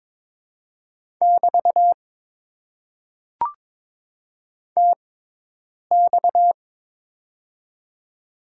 Morse code, heard as a keyed tone, about 22 wpm, spaced as for 8 wpm, 700 Hz. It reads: <BT> TX
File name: Top 90 QSO Elements - 6x Character Spacing - 22wpm